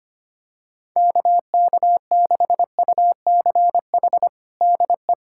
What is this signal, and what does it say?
Morse code, heard as a keyed tone, about 25 wpm, 700 Hz. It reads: KK6UCH DE
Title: USA Callsign DE Callsign 25wpm